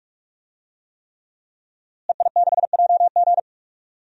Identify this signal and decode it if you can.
Morse code, heard as a keyed tone, about 46 wpm, 700 Hz. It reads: EI6JG
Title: DXCC-100 Call Signs - 5 Characters - Repeated 3x - 45wpm